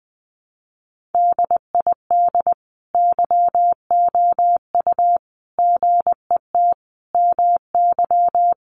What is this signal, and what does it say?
Morse code, heard as a keyed tone, about 20 wpm, 700 Hz. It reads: DID YOU GET MY
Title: Sentences from Top 1100 Words 20wpm